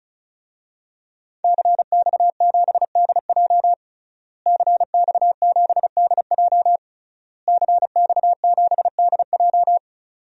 Morse code, heard as a keyed tone, about 34 wpm, 700 Hz. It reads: CX7DJ CX7DJ CX7DJ
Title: DXCC-100 Call Signs - 5 Characters - Repeated 3x - 35wpm